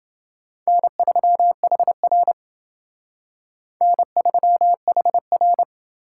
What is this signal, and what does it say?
Morse code, heard as a keyed tone, about 28 wpm, 700 Hz. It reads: N3HR N3HR